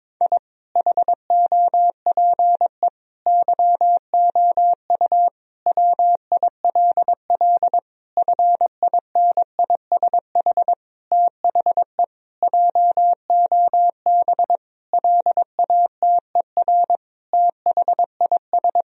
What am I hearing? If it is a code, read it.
Morse code, heard as a keyed tone, about 22 wpm, 700 Hz. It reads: I HOPE YOU WILL FINISH THE JOB LATER THIS
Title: Sentences from Top 1000 Words - Review Part 3 - 22wpm